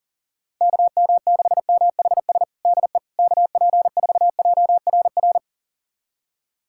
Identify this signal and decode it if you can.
Morse code, heard as a keyed tone, about 40 wpm, 700 Hz. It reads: KM6MHS DE KP4JRR
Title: USA Callsign DE Callsign - 6 Characters - 40wpm